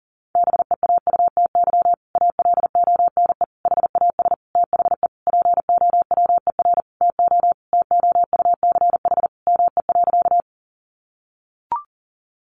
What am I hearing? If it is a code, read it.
Morse code, heard as a keyed tone, about 40 wpm, 700 Hz. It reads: BEAUTY ALONE HAS THE POWER TO TOUCH ME.
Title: Sentences from Top 1200 Words 40wpm